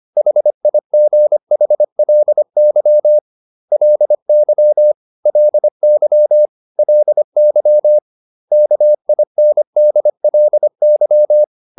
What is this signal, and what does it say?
Morse code, heard as a keyed tone, about 25 wpm, 600 Hz. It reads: HIGHLY LY LY LY KINDLY